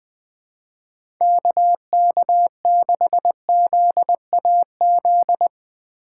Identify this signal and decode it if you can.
Morse code, heard as a keyed tone, about 20 wpm, 700 Hz. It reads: KK6ZAZ
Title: USA Call Signs - 3x Character Spacing - 20wpm